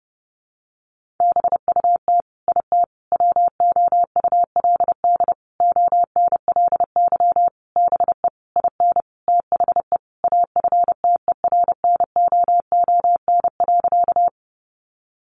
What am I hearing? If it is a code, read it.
Morse code, heard as a keyed tone, about 30 wpm, 700 Hz. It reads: BUT IT WOULD ONLY BE IN THE AFTERNOON.
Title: Sentences from Top 1500 Words 30wpm